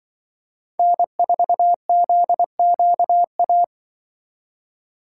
Morse code, heard as a keyed tone, about 25 wpm, 700 Hz. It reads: N4ZQA